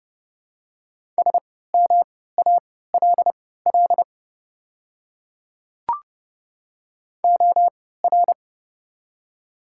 Morse code, heard as a keyed tone, about 30 wpm, spaced as for 17 wpm, 700 Hz. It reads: SMALL OR
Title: 3x Character Spacing - Top 200 Words - 30wpm